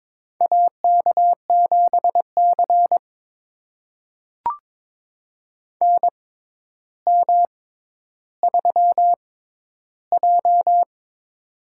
Morse code, heard as a keyed tone, about 22 wpm, spaced as for 11 wpm, 700 Hz. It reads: AK7C NM3J
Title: USA Call Signs - 6x Character Spacing - 22wpm